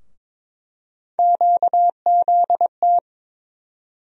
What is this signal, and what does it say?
Morse code, heard as a keyed tone, about 22 wpm, 700 Hz. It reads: QZT
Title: USA Call Sign Postfixes - 3 Letters 22wpm